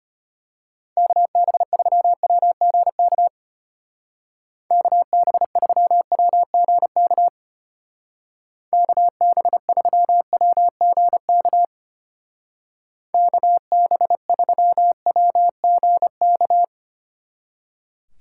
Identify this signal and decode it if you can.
Morse code, heard as a keyed tone, about 32 wpm, 700 Hz. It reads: KB3WGK KB3WGK KB3WGK KB3WGK